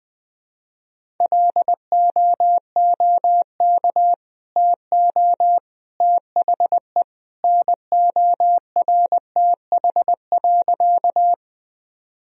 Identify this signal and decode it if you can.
Morse code, heard as a keyed tone, about 20 wpm, 700 Hz. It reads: LOOK TO THE NORTH.
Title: Sentences from Top 1000 Words - Review Part 1 - 20wpm